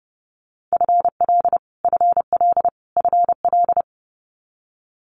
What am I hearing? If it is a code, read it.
Morse code, heard as a keyed tone, about 30 wpm, 700 Hz. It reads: FL FL FL
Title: US States Repeated 3x 30wpm